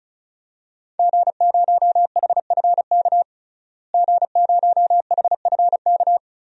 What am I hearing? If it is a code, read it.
Morse code, heard as a keyed tone, about 36 wpm, 700 Hz. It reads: G0HFK G0HFK
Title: DXCC-100 Call Signs - 5 Characters - Repeated 3x - 35wpm